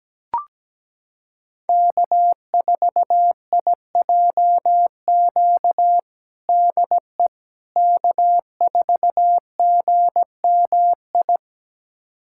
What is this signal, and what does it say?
Morse code, heard as a keyed tone, about 17 wpm, 700 Hz. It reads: K4IJQ DE K4GMI
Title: USA Callsign DE Callsign 17wpm